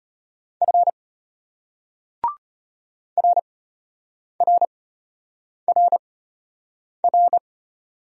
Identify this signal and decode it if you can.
Morse code, heard as a keyed tone, about 33 wpm, spaced as for 8 wpm, 700 Hz. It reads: F RRRR